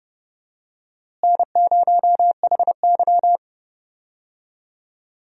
Morse code, heard as a keyed tone, about 32 wpm, 700 Hz. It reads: N0HY
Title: USA Call Signs - 4 Characters - Repeated 3x - 30wpm